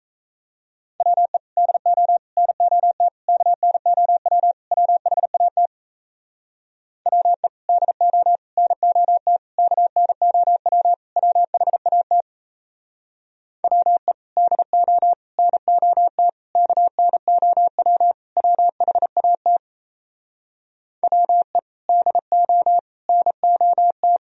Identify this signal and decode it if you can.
Morse code, heard as a keyed tone, about 37 wpm, 700 Hz. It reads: WE DO NOT KNOW WHAT WE DO NOT KNOW WHAT WE DO NOT KNOW WHAT WE DO NOT